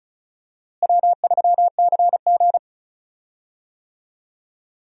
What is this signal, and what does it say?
Morse code, heard as a keyed tone, about 34 wpm, 700 Hz. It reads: W3CG
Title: USA Call Signs - 4 Characters - Repeated 3x - 35wpm